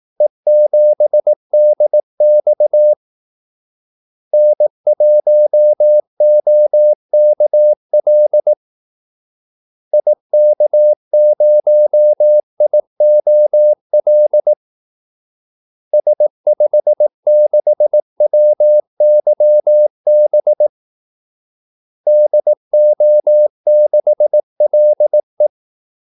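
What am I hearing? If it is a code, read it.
Morse code, heard as a keyed tone, about 18 wpm, 600 Hz. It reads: E7DX N1OKL IK0IOL S56WYB DO6LE
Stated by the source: Morse call-sign practice at 18 wpm.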